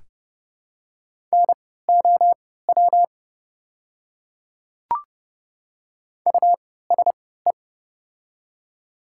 Morse code, heard as a keyed tone, about 30 wpm, spaced as for 17 wpm, 700 Hz. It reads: NOW USE
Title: 3x Character Spacing - Top 200 Words - 30wpm